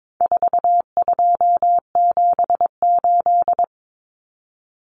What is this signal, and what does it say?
Morse code, heard as a keyed tone, about 22 wpm, 700 Hz. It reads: 4278
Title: Numbers - 4-Digit 22wpm